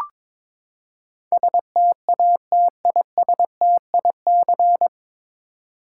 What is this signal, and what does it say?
Morse code, heard as a keyed tone, about 22 wpm, 700 Hz. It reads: STATISTIC
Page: Morse Code Ninja practice 22wpm